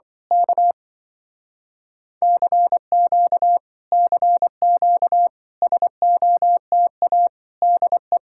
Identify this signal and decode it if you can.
Morse code, heard as a keyed tone, about 24 wpm, 700 Hz. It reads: K CQ CQ SOTA DE
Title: Speed-Racing Common QSO Phrases - 20wpm